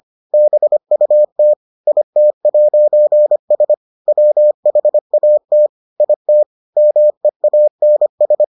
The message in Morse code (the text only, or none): BUT IT'S WHAT IT MEANS